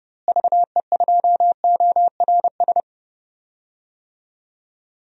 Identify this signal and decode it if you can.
Morse code, heard as a keyed tone, about 30 wpm, 700 Hz. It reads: VE2ORS
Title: DXCC-100 Call Signs - 6 Characters - Repeated 3x - 30wpm